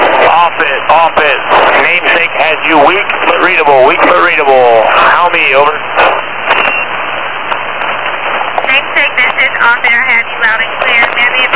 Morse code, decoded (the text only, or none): none